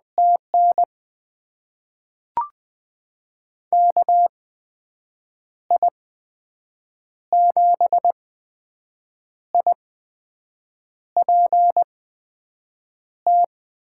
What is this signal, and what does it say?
Morse code, heard as a keyed tone, about 20 wpm, spaced as for 6 wpm, 700 Hz. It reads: TN KI7IPT
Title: USA Call Signs - 6 Characters - 8x Character Spacing - 20wpm